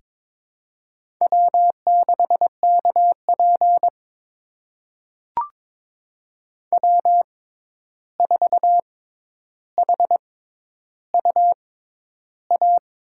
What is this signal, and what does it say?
Morse code, heard as a keyed tone, about 22 wpm, spaced as for 11 wpm, 700 Hz. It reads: W6KP W4HUA
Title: USA Call Signs - 6x Character Spacing - 22wpm